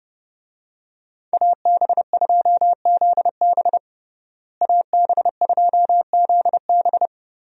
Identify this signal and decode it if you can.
Morse code, heard as a keyed tone, about 30 wpm, 700 Hz. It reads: AB2ZB AB2ZB
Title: USA Call Signs - 5 Characters - Repeated 3x - 30wpm